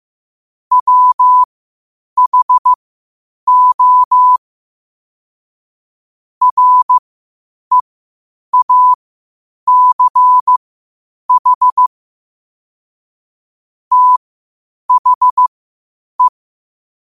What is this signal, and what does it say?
Morse code, heard as a keyed tone, about 15 wpm, spaced as for 8 wpm, 1000 Hz. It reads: WHO REACH THE
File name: qotdfiles / 8wpm